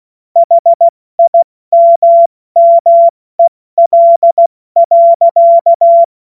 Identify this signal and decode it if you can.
Morse code, heard as a keyed tone, about 16 wpm, 680 Hz. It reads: HIMMEL.
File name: Morsekurs / Witze / Witz 02. Februar 2021, 70 F80